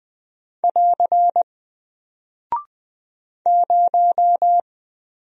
Morse code, heard as a keyed tone, about 20 wpm, 700 Hz. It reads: <AR> 0
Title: Single Character (Letter:Number:Punctuation:Prosign) 20wpm